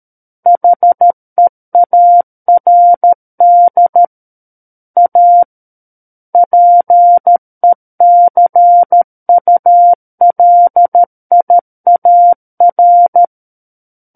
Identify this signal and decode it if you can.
Morse code, heard as a keyed tone, about 13 wpm, 700 Hz. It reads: HEARD A PECULIAR
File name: Book One / 04-Chapter 4 - 13 WPM